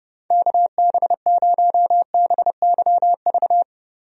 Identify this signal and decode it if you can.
Morse code, heard as a keyed tone, about 30 wpm, 700 Hz. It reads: KB0BYV
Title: USA Call Signs - 6 Characters - Repeated 3x - 30wpm